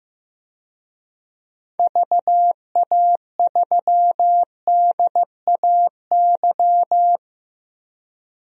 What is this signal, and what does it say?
Morse code, heard as a keyed tone, about 15 wpm, 700 Hz. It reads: VA3DAY
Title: DXCC-100 Call Signs - 6 Characters - Repeated 3x - 15wpm